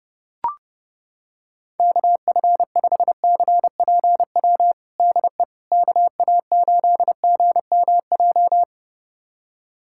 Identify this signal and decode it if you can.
Morse code, heard as a keyed tone, about 32 wpm, 700 Hz. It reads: KF5CPW DE KA8GMJ
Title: USA Callsign DE Callsign - 6 Characters - 30wpm